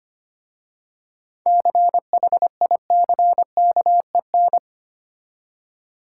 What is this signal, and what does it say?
Morse code, heard as a keyed tone, about 25 wpm, 690 Hz. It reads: CHICKEN
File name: practice / 25wpm / Easy Words with Hints - 25wpm